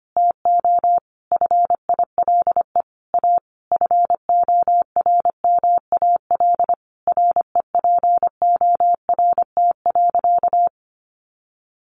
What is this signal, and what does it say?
Morse code, heard as a keyed tone, about 25 wpm, 700 Hz. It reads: TO FILE A FORMAL REPORT.
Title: Sentences from Top 1300 Words 25wpm